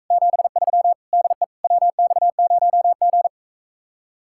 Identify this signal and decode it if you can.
Morse code, heard as a keyed tone, about 45 wpm, 700 Hz. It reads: 73 DE WX0G